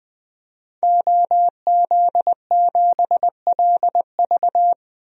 Morse code, heard as a keyed tone, about 20 wpm, 700 Hz. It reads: OZ7LV
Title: DXCC-100 Call Signs - 5 Characters - 20wpm